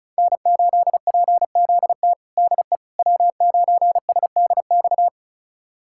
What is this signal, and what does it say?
Morse code, heard as a keyed tone, about 36 wpm, 700 Hz. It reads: N8PZT DE W9SDX
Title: USA Callsign DE Callsign - 5 Characters - 35wpm